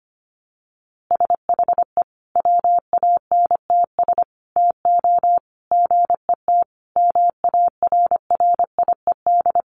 Morse code, heard as a keyed tone, about 25 wpm, 700 Hz. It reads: SHE WANTS TO GET MARRIED
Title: Easy Phrases 25wpm